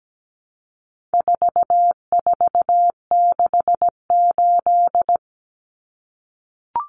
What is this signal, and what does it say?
Morse code, heard as a keyed tone, about 17 wpm, 700 Hz. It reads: 4468
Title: Numbers - 4-Digit 17wpm